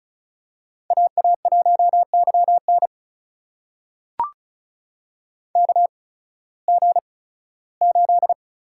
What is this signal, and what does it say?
Morse code, heard as a keyed tone, about 36 wpm, spaced as for 19 wpm, 700 Hz. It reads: AA1YN KG8
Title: USA Call Signs - 5 Characters - 8x Character Spacing - 35wpm